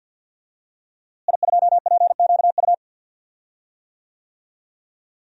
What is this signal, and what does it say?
Morse code, heard as a keyed tone, about 52 wpm, 700 Hz. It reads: I2PQU
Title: DXCC-100 Call Signs - 5 Characters - Repeated 3x - 50wpm